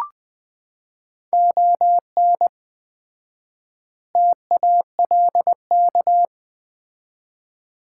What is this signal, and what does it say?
Morse code, heard as a keyed tone, about 20 wpm, 700 Hz. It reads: ON TALK
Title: 4x Word-Spacing - Sets of 5 Words - 20wpm